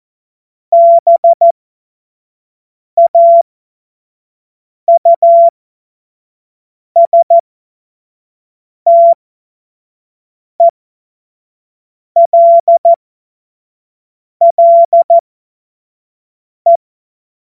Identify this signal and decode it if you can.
Morse code, heard as a keyed tone, about 14 wpm, spaced as for 5 wpm, 680 Hz. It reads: BAUSTELLE
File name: Morsekurs / Witze / Witz 09. Dezember 2025 25 F70